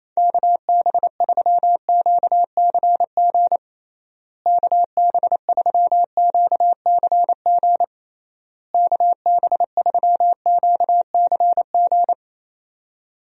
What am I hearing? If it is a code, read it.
Morse code, heard as a keyed tone, about 28 wpm, 700 Hz. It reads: KB3QCG KB3QCG KB3QCG
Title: USA Call Signs - 6 Characters - Repeated 3x - 28wpm